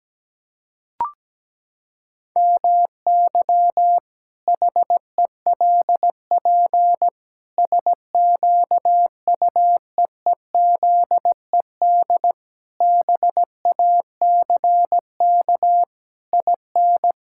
Morse code, heard as a keyed tone, about 17 wpm, 700 Hz. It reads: MY HELP SQUEEZED BACK IN
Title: Pangram Challenge v2 - 17wpm